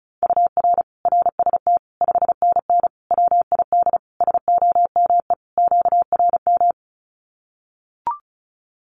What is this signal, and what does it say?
Morse code, heard as a keyed tone, about 34 wpm, 700 Hz. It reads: UR RST 5NN WID SOME QRM
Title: Common QSO Phrases 35wpm